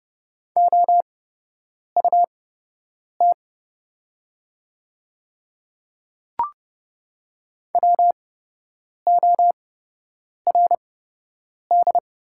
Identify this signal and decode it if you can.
Morse code, heard as a keyed tone, about 30 wpm, spaced as for 8 wpm, 700 Hz. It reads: OUT WORD
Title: 8x Character Spacing - Top 100 Words - 30wpm